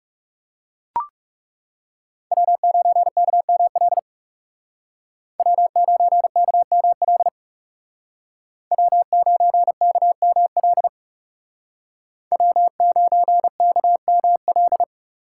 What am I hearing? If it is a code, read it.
Morse code, heard as a keyed tone, about 36 wpm, 700 Hz. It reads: W9KML W9KML W9KML W9KML
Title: Speed Racing - USA 1x3 Call Signs 30wpm